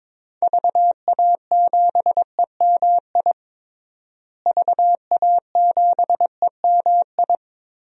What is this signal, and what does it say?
Morse code, heard as a keyed tone, about 21 wpm, 700 Hz. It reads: VA7EMI VA7EMI
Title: DXCC-100 Call Signs - 6 Characters - Repeated 2x - 22wpm